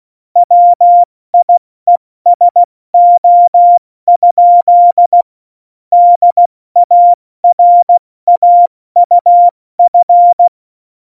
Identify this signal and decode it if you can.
Morse code, heard as a keyed tone, about 16 wpm, 700 Hz. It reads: WIESO? DARAUF
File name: Morsekurs / Witze / Witz 17.Juli 2018, 70 F80